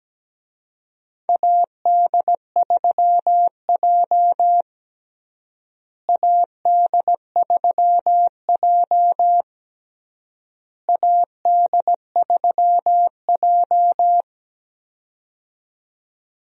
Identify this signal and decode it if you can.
Morse code, heard as a keyed tone, about 17 wpm, 700 Hz. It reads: AD3J AD3J AD3J
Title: USA Call Signs - 4 Characters - Repeated 3x - 17wpm